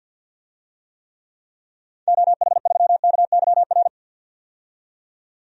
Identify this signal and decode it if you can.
Morse code, heard as a keyed tone, about 52 wpm, 700 Hz. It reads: OH3KYR
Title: DXCC-100 Call Signs - 6 Characters - 50wpm